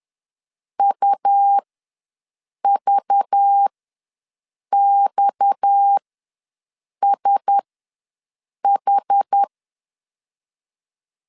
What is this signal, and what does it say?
Morse code, heard as a keyed tone, about 10 wpm, spaced as for 6 wpm, 800 Hz. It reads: UVXSH
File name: LEZIONI / 6 PKX0 / 6 PKX ws 10wpm 3706